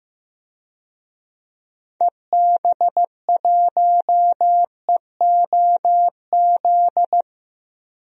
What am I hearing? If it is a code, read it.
Morse code, heard as a keyed tone, about 15 wpm, 700 Hz. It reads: EB1EOZ
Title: DXCC-100 Call Signs - 6 Characters - 15wpm